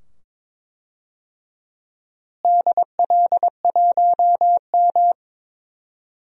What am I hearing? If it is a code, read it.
Morse code, heard as a keyed tone, about 22 wpm, 700 Hz. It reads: DL1M
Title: DXCC-100 Call Signs - 4 Characters - Repeated 2x - 22wpm